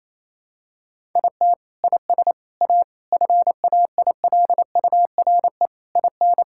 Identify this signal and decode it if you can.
Morse code, heard as a keyed tone, about 28 wpm, 700 Hz. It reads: IT IS A FAILURE IN
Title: Sentences from Top 1000 Words - Review Part 6 - 28wpm